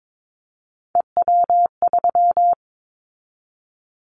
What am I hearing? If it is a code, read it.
Morse code, heard as a keyed tone, about 22 wpm, 700 Hz. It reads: EW3
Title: Callsign Prefixes for Top 100 DXCC - 22wpm